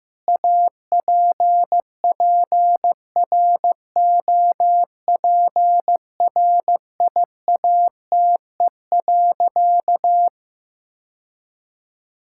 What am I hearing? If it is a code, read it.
Morse code, heard as a keyed tone, about 15 wpm, 700 Hz. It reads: APPROPRIATE.
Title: Sentences from Top 1000 Words - Review Part 5 - 15wpm